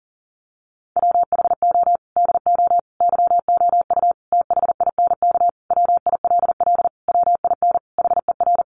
Morse code, heard as a keyed tone, about 40 wpm, 700 Hz. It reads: WHO DO YOU THINK WILL WIN HER
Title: Sentences from Top 1500 Words 40wpm